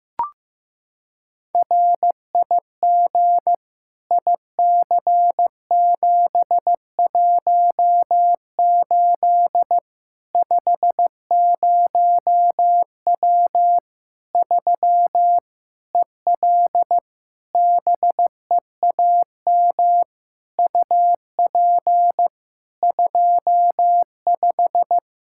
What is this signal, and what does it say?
Morse code, heard as a keyed tone, about 15 wpm, 700 Hz. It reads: RIG IC718 50W 3 EL BEAM UP 25
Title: Practice QSO - Exchange 2 - Info - 15wpm